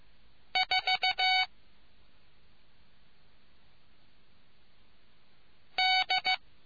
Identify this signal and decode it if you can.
Morse code, heard as a keyed tone, about 15 wpm, 750 Hz. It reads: 4 D